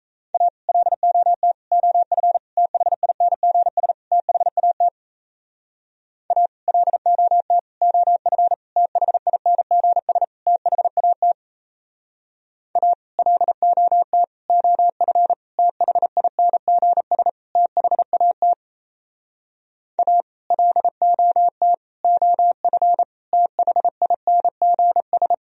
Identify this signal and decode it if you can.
Morse code, heard as a keyed tone, about 37 wpm, 700 Hz. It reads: A LOT OF THINGS THAT A LOT OF THINGS THAT A LOT OF THINGS THAT A LOT OF THINGS